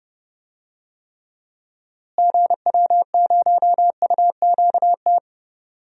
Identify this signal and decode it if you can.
Morse code, heard as a keyed tone, about 30 wpm, 700 Hz. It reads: GW0UQT
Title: DXCC-100 Call Signs - 6 Characters - Repeated 2x - 30wpm